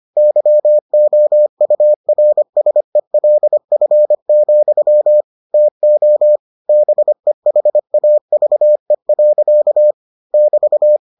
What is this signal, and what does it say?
Morse code, heard as a keyed tone, about 25 wpm, 600 Hz. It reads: YOURSELF, TO BEHAVE. <BT>